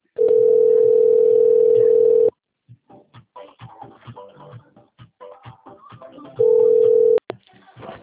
Ringback tone (North America), 440 + 480 Hz.